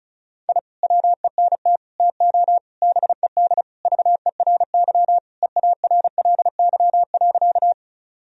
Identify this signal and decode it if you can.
Morse code, heard as a keyed tone, about 34 wpm, 700 Hz. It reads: I WENT TO BED VERY EARLY.
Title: Sentences from Top 1000 Words - Review Part 2 - 35wpm